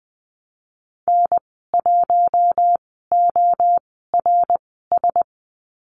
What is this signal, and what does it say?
Morse code, heard as a keyed tone, about 20 wpm, spaced as for 15 wpm, 700 Hz. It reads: N1ORS
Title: 069 Lesson 12g - Letter d - Call Signs 20wpm